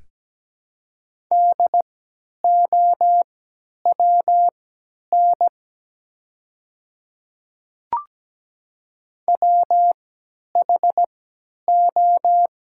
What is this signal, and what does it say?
Morse code, heard as a keyed tone, about 17 wpm, spaced as for 10 wpm, 700 Hz. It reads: DOWN WHO
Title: 3x Character Spacing - Top 100 Words - 17wpm